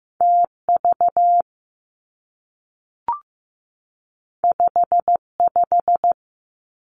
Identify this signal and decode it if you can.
Morse code, heard as a keyed tone, about 15 wpm, 700 Hz. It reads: TV 55
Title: Two Challenging Letter-Number Combinations 15wpm